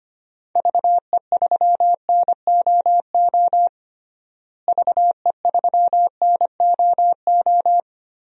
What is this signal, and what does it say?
Morse code, heard as a keyed tone, about 25 wpm, 700 Hz. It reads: VE3NOO VE3NOO
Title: DXCC-100 Call Signs - 6 Characters - Repeated 2x - 25wpm